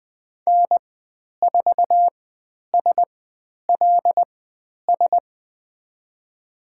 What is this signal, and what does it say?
Morse code, heard as a keyed tone, about 20 wpm, spaced as for 10 wpm, 700 Hz. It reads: N4SLS
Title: 095 Lesson 16g - Letter c - Call Signs 20wpm